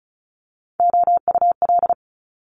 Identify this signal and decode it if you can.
Morse code, heard as a keyed tone, about 34 wpm, 700 Hz. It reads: OUL
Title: Multi-letter Phonemes Quick Fire 35wpm